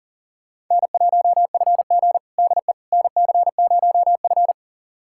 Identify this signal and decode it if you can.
Morse code, heard as a keyed tone, about 40 wpm, 700 Hz. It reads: N1FG DE NC0F
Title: USA Callsign DE Callsign - 4 Characters - 40wpm